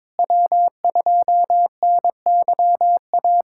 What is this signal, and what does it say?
Morse code, heard as a keyed tone, about 22 wpm, 700 Hz. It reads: W2NYA
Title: USA Callsign DE Callsign - 5 Characters - 22wpm